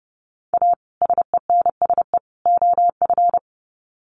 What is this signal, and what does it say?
Morse code, heard as a keyed tone, about 30 wpm, 700 Hz. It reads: A SENSE OF
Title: Sets of 3 Words 30wpm